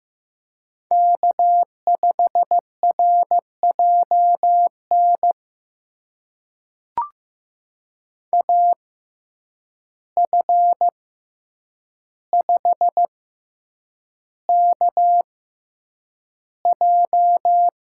Morse code, heard as a keyed tone, about 15 wpm, spaced as for 8 wpm, 700 Hz. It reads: K5RJN AF5KJ
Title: USA Call Signs - 5 Characters - 6x Character Spacing - 15wpm